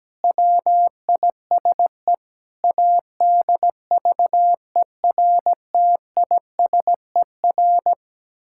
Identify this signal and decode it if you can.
Morse code, heard as a keyed tone, about 17 wpm, 700 Hz. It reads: WISE ADVERTISER
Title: Pangram Challenge v2 - 17wpm